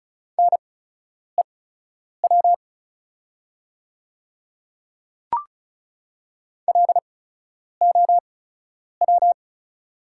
Morse code, heard as a keyed tone, about 35 wpm, spaced as for 10 wpm, 690 Hz. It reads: NEW LOW
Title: 8x Character Spacing - Top 300 Words - 35wpm